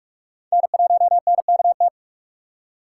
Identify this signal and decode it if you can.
Morse code, heard as a keyed tone, about 47 wpm, 700 Hz. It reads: N1NKT
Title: USA Call Signs - 6x Character Spacing - 45wpm